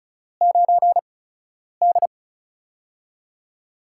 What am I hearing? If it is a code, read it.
Morse code, heard as a keyed tone, about 34 wpm, spaced as for 10 wpm, 700 Hz. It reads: 9D
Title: USA Call Signs - 4 Characters - 8x Character Spacing - 35wpm